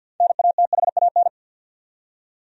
Morse code, heard as a keyed tone, about 50 wpm, 700 Hz. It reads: NATHAN
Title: USA Common Names - 50wpm